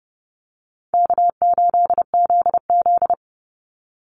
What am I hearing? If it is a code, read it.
Morse code, heard as a keyed tone, about 30 wpm, 700 Hz. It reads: K8ZZ
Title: Easy Call Signs - 30wpm